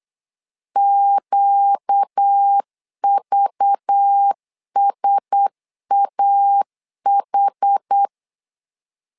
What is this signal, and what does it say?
Morse code, heard as a keyed tone, about 8 wpm, 800 Hz. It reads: QVSAH
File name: LEZIONI / 5 LBVC / 5 LBV 8wpm 2837